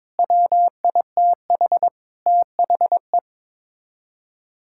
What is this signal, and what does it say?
Morse code, heard as a keyed tone, about 22 wpm, 700 Hz. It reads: WITH THE
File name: practice / 22wpm / Sets of 6 Words - 22wpm